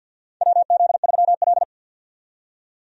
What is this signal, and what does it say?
Morse code, heard as a keyed tone, about 51 wpm, 700 Hz. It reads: WZ3L
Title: USA Call Signs - 4 Characters - 50wpm